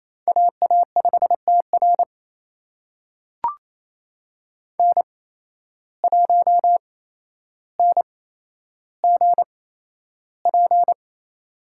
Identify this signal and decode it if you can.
Morse code, heard as a keyed tone, about 28 wpm, spaced as for 12 wpm, 700 Hz. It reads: AA5TR N1NGP
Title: USA Call Signs - 8x Character Spacing - 28wpm